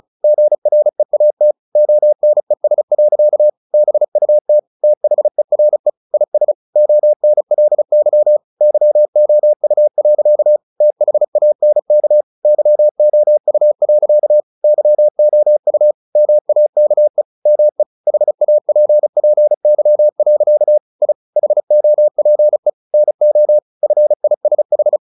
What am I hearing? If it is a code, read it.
Morse code, heard as a keyed tone, about 35 wpm, 600 Hz. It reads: GREAT ONES. BUT THERE IS ONLY YOU. THANK YOU. YOU MAKE ME HAPPY. I HOPE NO FISH
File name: the old man and the sea / the old man and the sea 1 35wpm